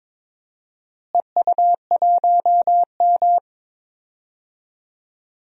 Morse code, heard as a keyed tone, about 22 wpm, 700 Hz. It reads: EU1M
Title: DXCC-100 Call Signs - 4 Characters - Repeated 2x - 22wpm